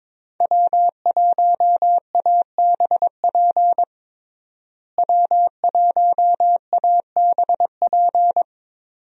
Morse code, heard as a keyed tone, about 22 wpm, 700 Hz. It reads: W1ABP W1ABP
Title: USA Call Signs - 5 Characters - Repeated 2x - 22wpm